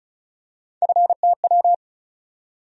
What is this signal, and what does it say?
Morse code, heard as a keyed tone, about 36 wpm, 700 Hz. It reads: FTW
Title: USA Call Sign Postfixes - 3 Letters 35wpm